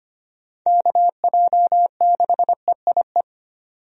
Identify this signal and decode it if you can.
Morse code, heard as a keyed tone, about 25 wpm, 700 Hz. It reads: KJ6EIE
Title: USA Call Signs - 6 Characters - 25wpm